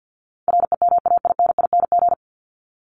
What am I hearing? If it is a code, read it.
Morse code, heard as a keyed tone, about 48 wpm, 700 Hz. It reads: REMAINING
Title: ING Suffix - Top 100 Words with Base Words 50wpm